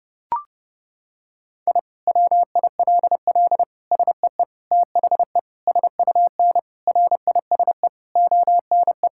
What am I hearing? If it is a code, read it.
Morse code, heard as a keyed tone, about 30 wpm, 700 Hz. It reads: I WILL SEE THE SUN RISE ONE
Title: Sentences from Top 1000 Words - Review Part 6 - 30wpm